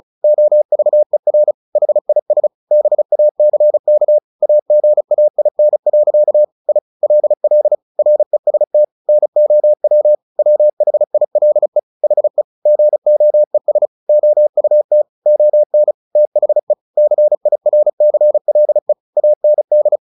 Morse code, heard as a keyed tone, about 34 wpm, 600 Hz. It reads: OVER HIS BACK AGAIN. I LL REST NOW WHILE HE GOES OUT ON THE CIRCLE AND
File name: the old man and the sea 2 35wpm